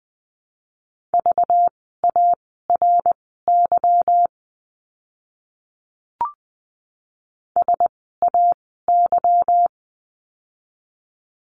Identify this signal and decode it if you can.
Morse code, heard as a keyed tone, about 20 wpm, spaced as for 15 wpm, 700 Hz. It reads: VARY SAY
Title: 192 Lesson 31f - Letter v - Words 20wpm